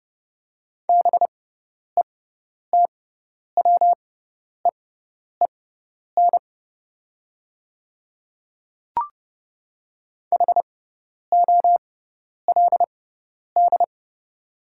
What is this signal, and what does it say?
Morse code, heard as a keyed tone, about 30 wpm, spaced as for 10 wpm, 690 Hz. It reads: BETWEEN HOLD
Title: 6x Character Spacing - Top 500 Words - 30wpm